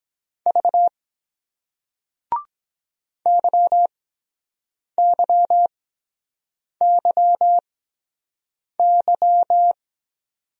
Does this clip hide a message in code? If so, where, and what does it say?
Yes: Morse code from 0.5 to 9.7 s, heard as a keyed tone, about 23 wpm, spaced as for 7 wpm, 700 Hz. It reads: V YYYY